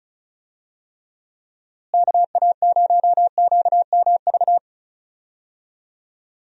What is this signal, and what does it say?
Morse code, heard as a keyed tone, about 34 wpm, 700 Hz. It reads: KA0QMV
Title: USA Call Signs - 6 Characters - Repeated 3x - 35wpm